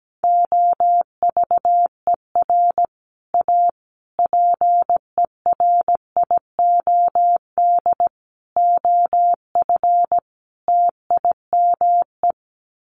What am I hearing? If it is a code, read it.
Morse code, heard as a keyed tone, about 17 wpm, 700 Hz. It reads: OVER A PERIOD OF TIME
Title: Sets of 5 Words Encore2 17wpm